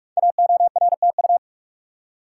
Morse code, heard as a keyed tone, about 44 wpm, 690 Hz. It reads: AORTU
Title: Code Groups - 5 Letters - 45wpm